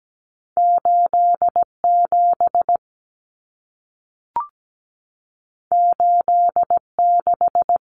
Morse code, heard as a keyed tone, about 17 wpm, 700 Hz. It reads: 87 86
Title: Common Cut Numbers - 2-Digit 17wpm